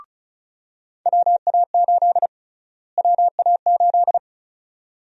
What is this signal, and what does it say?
Morse code, heard as a keyed tone, about 36 wpm, 700 Hz. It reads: WA8 WA8
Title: USA Call Sign Prefixes - Repeated 2x - 35wpm